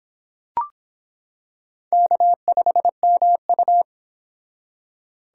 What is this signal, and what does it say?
Morse code, heard as a keyed tone, about 25 wpm, 700 Hz. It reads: K5MU